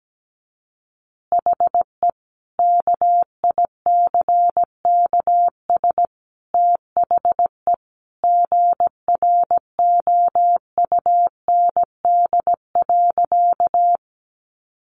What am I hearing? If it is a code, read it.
Morse code, heard as a keyed tone, about 17 wpm, 700 Hz. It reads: HE KICKS THE GROUND.
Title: Additional Easy Phrases 17wpm